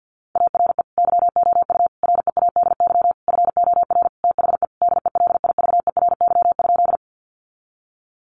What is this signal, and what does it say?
Morse code, heard as a keyed tone, about 49 wpm, 700 Hz. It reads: ARE YOU READY FOR THE DELIVERY?
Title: Sentences from Top 1500 Words 50wpm